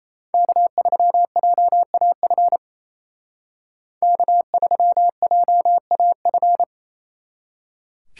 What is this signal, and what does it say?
Morse code, heard as a keyed tone, about 33 wpm, 700 Hz. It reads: K3JAF K3JAF
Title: Speed Racing - USA Call Signs 28wpm